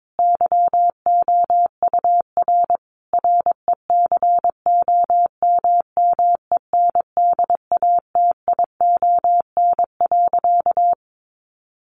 Morse code, heard as a keyed tone, about 22 wpm, 700 Hz. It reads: YOUR RECOMMENDATION.
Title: Sentences from Top 1400 Words 22wpm